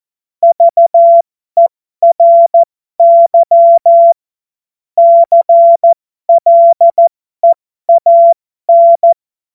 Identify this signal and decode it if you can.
Morse code, heard as a keyed tone, about 14 wpm, spaced as for 12 wpm, 680 Hz. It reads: VERY CLEAN
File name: Morsekurs / Witze